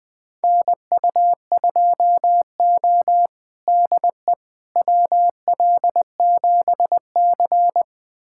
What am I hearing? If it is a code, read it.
Morse code, heard as a keyed tone, about 20 wpm, 700 Hz. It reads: NU2O DE WL7C
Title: USA Callsign DE Callsign - 4 Characters - 20wpm